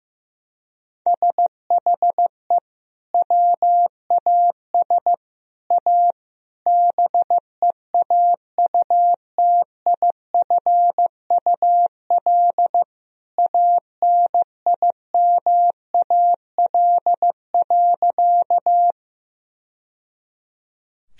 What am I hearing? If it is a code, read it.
Morse code, heard as a keyed tone, about 16 wpm, 700 Hz. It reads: SHE WAS A BEAUTIFUL ANIMAL.
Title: Sentences from Top 1000 Words - Review Part 1 - 15wpm